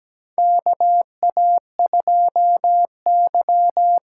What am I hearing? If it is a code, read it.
Morse code, heard as a keyed tone, about 17 wpm, 700 Hz. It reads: KA2Y
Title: USA Call Signs - 6 Characters - 8x Character Spacing - 17wpm